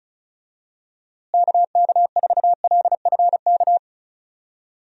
Morse code, heard as a keyed tone, about 36 wpm, 700 Hz. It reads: KK4LFK
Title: USA Call Signs - 6 Characters - Repeated 3x - 35wpm